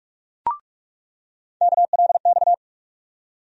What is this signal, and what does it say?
Morse code, heard as a keyed tone, about 43 wpm, 700 Hz. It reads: KLX
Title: USA Call Sign Postfixes - 3 Letters 45wpm